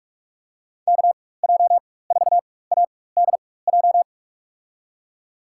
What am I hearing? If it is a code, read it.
Morse code, heard as a keyed tone, about 47 wpm, spaced as for 21 wpm, 700 Hz. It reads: KJ4ADJ
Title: USA Call Signs - 6 Characters - 4x Character Spacing - 45wpm